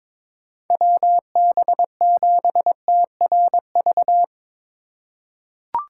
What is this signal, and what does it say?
Morse code, heard as a keyed tone, about 21 wpm, 700 Hz. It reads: WB7TRV
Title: USA Call Signs - 2x Character Spacing - 22wpm